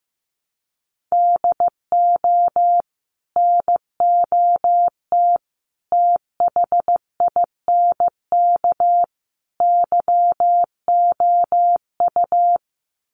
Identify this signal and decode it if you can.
Morse code, heard as a keyed tone, about 15 wpm, 700 Hz. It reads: DO NOT THINK YOU
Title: Sets of 4 Words 15wpm